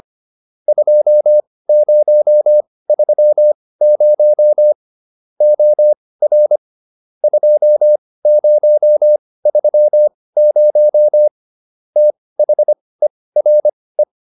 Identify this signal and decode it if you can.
Morse code, heard as a keyed tone, about 25 wpm, spaced as for 18 wpm, 600 Hz. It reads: 2030 OR 2030 THERE